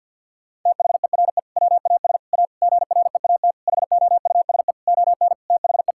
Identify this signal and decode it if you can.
Morse code, heard as a keyed tone, about 50 wpm, 700 Hz. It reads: THERE WAS A GREAT HOUSE ON THE
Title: Sentences from Top 1600 Words 50wpm